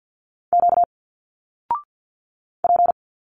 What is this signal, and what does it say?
Morse code, heard as a keyed tone, about 47 wpm, 700 Hz. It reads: Q P
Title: Single Letters 50wpm